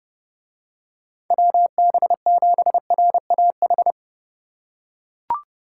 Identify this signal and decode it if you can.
Morse code, heard as a keyed tone, about 28 wpm, 700 Hz. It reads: WB7RAH